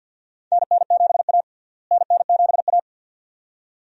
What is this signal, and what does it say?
Morse code, heard as a keyed tone, about 52 wpm, 700 Hz. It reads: NN7A NN7A
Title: USA Call Signs - 4 Characters - Repeated 2x - 50wpm